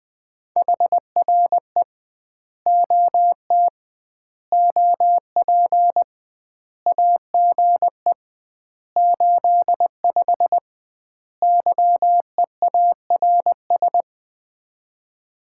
Morse code, heard as a keyed tone, about 20 wpm, 700 Hz. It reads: HRE OT OP AGE 85 YEARS
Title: Common QSO Phrases - 2x Word-Spacing - 20wpm